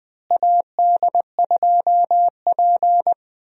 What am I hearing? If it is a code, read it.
Morse code, heard as a keyed tone, about 20 wpm, 700 Hz. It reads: AD2P
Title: USA Call Signs - 4 Characters - Repeated 3x - 20wpm